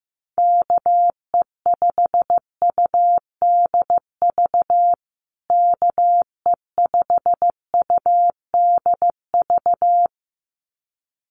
Morse code, heard as a keyed tone, about 14 wpm, 700 Hz. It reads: KE5UDV KE5UDV
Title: USA Call Signs Repeated 2x 15wpm